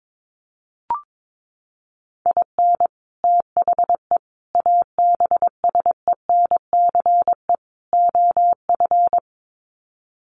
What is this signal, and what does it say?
Morse code, heard as a keyed tone, about 22 wpm, 700 Hz. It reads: IN THE ABSENCE OF
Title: Sets of 4 Words 22wpm